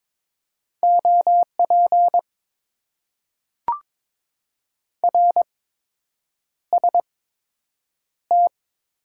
Morse code, heard as a keyed tone, about 22 wpm, spaced as for 8 wpm, 700 Hz. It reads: OP RST
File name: Top 90 QSO Elements - 8x Character Spacing - 22wpm